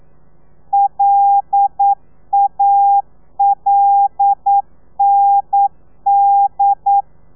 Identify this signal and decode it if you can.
Morse code, heard as a keyed tone, about 9 wpm, 800 Hz. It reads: LALND
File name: telegrafia / i2tfj / Lez5 40